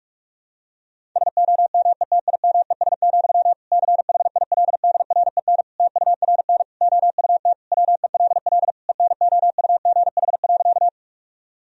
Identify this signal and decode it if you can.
Morse code, heard as a keyed tone, about 46 wpm, 700 Hz. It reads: SOMETIMES, CHILDREN TURN OUT WELL ENOUGH.
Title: Sentences from Top 1000 Words - Review Part 3 - 45wpm